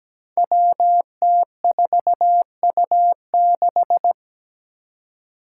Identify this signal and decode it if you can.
Morse code, heard as a keyed tone, about 17 wpm, 700 Hz. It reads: WT4U6
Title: Five Challenging Letter-Number Combinations 17wpm